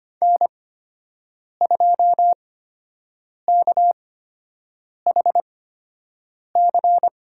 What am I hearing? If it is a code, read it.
Morse code, heard as a keyed tone, about 25 wpm, spaced as for 7 wpm, 700 Hz. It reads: N2KHC
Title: USA Call Signs - 8x Character Spacing - 25wpm